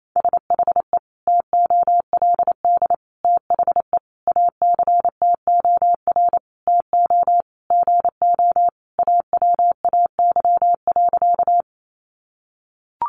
Morse code, heard as a keyed tone, about 28 wpm, 700 Hz. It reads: SHE TOLD THE ACTOR TO GO AWAY.
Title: Sentences from Top 1500 Words 28wpm